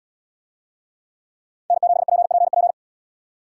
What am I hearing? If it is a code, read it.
Morse code, heard as a keyed tone, about 77 wpm, 700 Hz. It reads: N7PPP